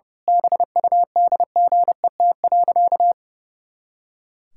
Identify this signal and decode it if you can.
Morse code, heard as a keyed tone, about 30 wpm, 700 Hz. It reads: BUDGET.
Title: Pangram Challenge v2 - 30wpm